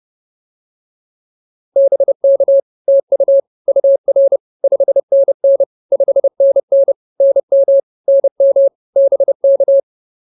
Morse code, heard as a keyed tone, about 30 wpm, 590 Hz. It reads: BK TU UR 5NN 5NN NM NM BK